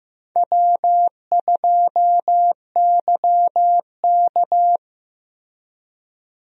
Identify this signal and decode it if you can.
Morse code, heard as a keyed tone, about 15 wpm, 700 Hz. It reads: W2YK
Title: USA Call Signs - 4 Characters - Repeated 2x - 15wpm